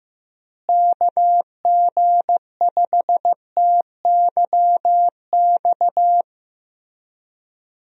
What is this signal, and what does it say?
Morse code, heard as a keyed tone, about 15 wpm, 700 Hz. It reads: KG5TYX
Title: USA Call Signs - 6 Characters - Repeated 3x - 15wpm